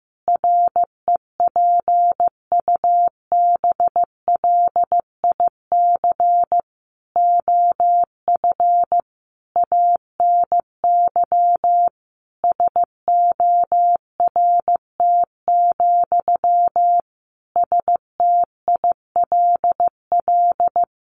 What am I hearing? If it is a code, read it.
Morse code, heard as a keyed tone, about 15 wpm, 700 Hz. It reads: REPUBLIC OF ANY SORT, STILL
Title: Sentences from Top 1500 Words 15wpm